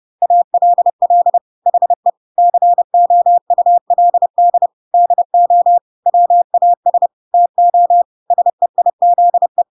Morse code, heard as a keyed tone, about 30 wpm, 700 Hz. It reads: ALL HE COULD DO WAS TO SEIZE
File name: Pinocchio / Pinocchio ch3 30 30 ik1hge